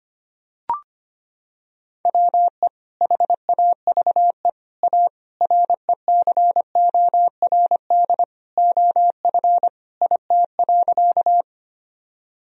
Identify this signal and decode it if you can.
Morse code, heard as a keyed tone, about 25 wpm, 700 Hz. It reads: WE HAVE A RECORD OF IT.
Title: Sentences from Top 1000 Words - Review Part 3 - 25wpm